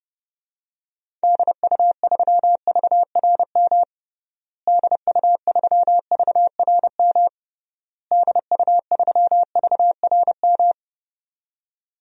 Morse code, heard as a keyed tone, about 30 wpm, 700 Hz. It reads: DU3VRM DU3VRM DU3VRM
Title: DXCC-100 Call Signs - 6 Characters - Repeated 3x - 30wpm